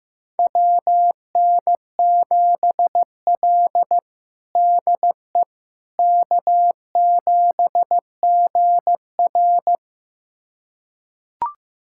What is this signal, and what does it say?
Morse code, heard as a keyed tone, about 15 wpm, 700 Hz. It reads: WN7L DE K7GR
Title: USA Callsign DE Callsign - 4 Characters - 15wpm